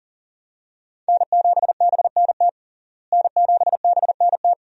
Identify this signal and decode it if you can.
Morse code, heard as a keyed tone, about 37 wpm, 700 Hz. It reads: N7BNT N7BNT
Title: USA Call Signs - 5 Characters - Repeated 2x - 40wpm